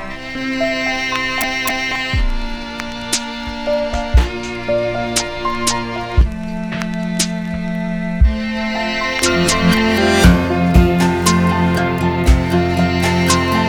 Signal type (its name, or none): dial-up modem handshake